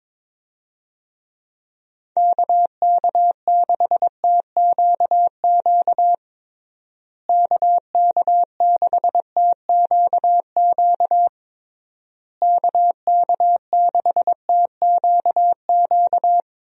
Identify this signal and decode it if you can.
Morse code, heard as a keyed tone, about 22 wpm, 700 Hz. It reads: KK6TQQ KK6TQQ KK6TQQ
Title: USA Call Signs - 6 Characters - Repeated 3x - 22wpm